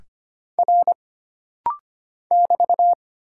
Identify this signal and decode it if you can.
Morse code, heard as a keyed tone, about 25 wpm, 700 Hz. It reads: R <BT>
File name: Single Character (Letter:Number:Punctuation:Prosign) - Rapid-Fire - 25wpm